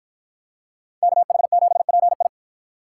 Morse code, heard as a keyed tone, about 53 wpm, 700 Hz. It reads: KH7PI